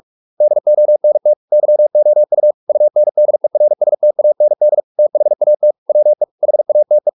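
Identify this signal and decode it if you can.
Morse code, heard as a keyed tone, about 44 wpm, 600 Hz. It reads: DONT YOU UNDERSTAND THAT WE HATE
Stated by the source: Morse practice at 45 wpm, the novel Never by Ken Follett in English, from CW med Gnister.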